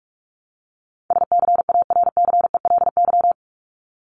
Morse code, heard as a keyed tone, about 45 wpm, 700 Hz. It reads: SCARCELY
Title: LY Suffix - Top 100 Words with Base Words 45wpm